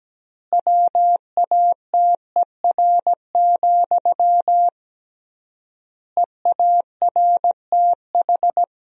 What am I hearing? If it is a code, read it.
Morse code, heard as a keyed tone, about 17 wpm, 700 Hz. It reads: WATER, EARTH
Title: 3x Word-Spacing - Sentences from Top 400 Words - 17wpm